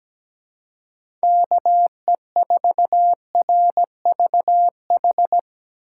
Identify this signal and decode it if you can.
Morse code, heard as a keyed tone, about 18 wpm, 700 Hz. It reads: KE4RVH
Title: Speed Racing - USA 2x3 Call Signs 17wpm